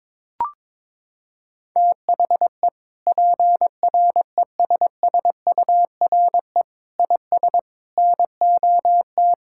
Morse code, heard as a keyed tone, about 22 wpm, 700 Hz. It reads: THE PRESSURE IS NOT
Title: Sentences from Top 1000 Words - Review Part 6 - 22wpm